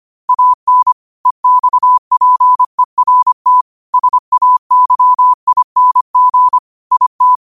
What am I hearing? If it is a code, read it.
Morse code, heard as a keyed tone, about 25 wpm, 1000 Hz. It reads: AN EXPERT SAYING IT